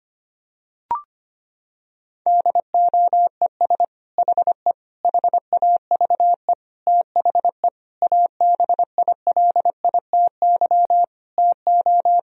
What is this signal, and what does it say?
Morse code, heard as a keyed tone, about 25 wpm, 700 Hz. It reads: DOES HE HAVE THE ABILITY TO
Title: Sentences from Top 1000 Words - Review Part 3 - 25wpm